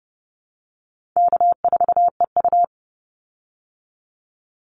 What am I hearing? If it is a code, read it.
Morse code, heard as a keyed tone, about 30 wpm, 700 Hz. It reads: K4EU
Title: Easy Call Signs Repeated 3x - 30wpm